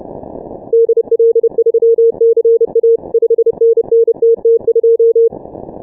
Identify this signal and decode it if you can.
Morse code, heard as a keyed tone, about 30 wpm, 460 Hz. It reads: DL3CA 5NN TT2
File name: dk5ke / Contest2QSOs 150BpM